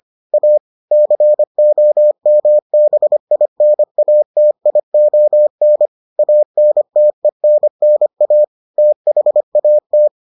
Morse code, heard as a keyed tone, about 25 wpm, 600 Hz. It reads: A COMBINATION ANTENNA THAT